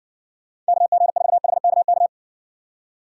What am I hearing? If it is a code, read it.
Morse code, heard as a keyed tone, about 60 wpm, 700 Hz. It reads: KG4HKK